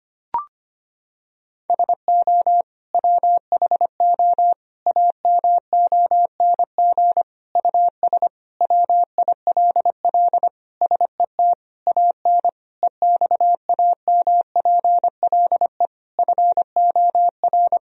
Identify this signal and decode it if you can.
Morse code, heard as a keyed tone, about 25 wpm, 700 Hz. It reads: SO WHO AMONG US WILL SET AN EXAMPLE FOR
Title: Sentences from Top 1000 Words - Review Part 3 - 25wpm